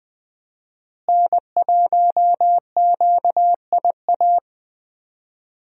Morse code, heard as a keyed tone, about 20 wpm, 700 Hz. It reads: N1QIA
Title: USA Call Signs - 5 Characters - Repeated 3x - 20wpm